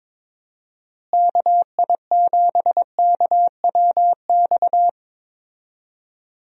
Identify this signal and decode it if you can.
Morse code, heard as a keyed tone, about 23 wpm, 700 Hz. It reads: KI7KWX
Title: USA Call Signs - 6 Characters - Repeated 3x - 22wpm